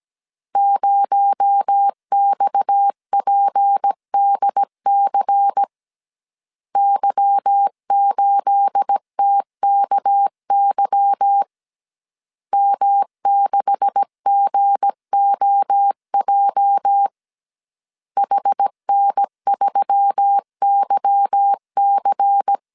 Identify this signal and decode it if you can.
Morse code, heard as a keyed tone, about 17 wpm, 800 Hz. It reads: 0XPDC Y8TKY M6GOJ HN3YC
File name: CW / DET / 1 9 1500C S A 16wpm 2828